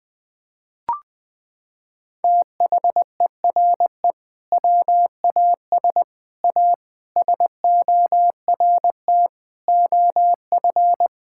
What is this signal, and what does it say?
Morse code, heard as a keyed tone, about 20 wpm, 700 Hz. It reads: THERE WAS A SORT OF
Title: Sentences from Top 1600 Words 20wpm